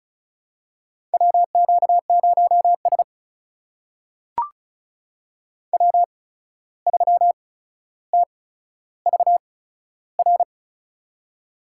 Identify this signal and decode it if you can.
Morse code, heard as a keyed tone, about 36 wpm, spaced as for 14 wpm, 700 Hz. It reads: WQ0S W3TVR
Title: USA Call Signs - 8x Character Spacing - 35wpm